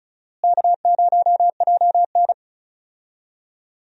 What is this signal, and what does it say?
Morse code, heard as a keyed tone, about 34 wpm, 700 Hz. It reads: K0JN
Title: USA Call Signs - 4 Characters - Repeated 3x - 35wpm